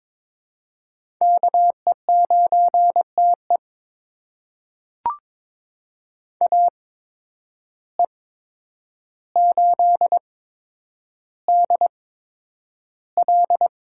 Morse code, heard as a keyed tone, about 22 wpm, spaced as for 9 wpm, 700 Hz. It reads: KE9TE AE8DL
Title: USA Call Signs - 5 Characters - 8x Character Spacing - 22wpm